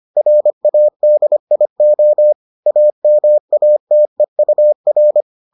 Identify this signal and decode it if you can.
Morse code, heard as a keyed tone, about 25 wpm, 600 Hz. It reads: RADIO AMATEUR